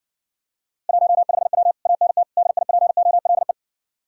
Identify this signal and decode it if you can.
Morse code, heard as a keyed tone, about 60 wpm, 700 Hz. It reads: 15W ANT DIPOLE